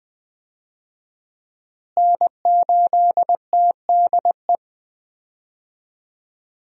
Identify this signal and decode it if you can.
Morse code, heard as a keyed tone, about 20 wpm, 700 Hz. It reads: N8TDE
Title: USA Call Signs - 5 Characters - Repeated 3x - 20wpm